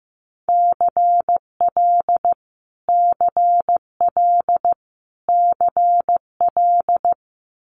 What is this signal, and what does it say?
Morse code, heard as a keyed tone, about 15 wpm, 700 Hz. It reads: CL CL CL
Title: Multi-letter Phonemes Repeated 3x 15wpm